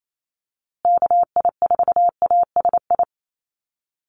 Morse code, heard as a keyed tone, about 28 wpm, 700 Hz. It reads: KI4ASI
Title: USA Call Signs Repeated 3x 28wpm